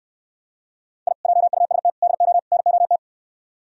Morse code, heard as a keyed tone, about 67 wpm, 700 Hz. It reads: I JUST DO NOT